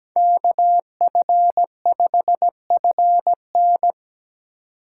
Morse code, heard as a keyed tone, about 20 wpm, 700 Hz. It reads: KF5FN
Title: Speed Racing - USA 2x2 Call Signs 17wpm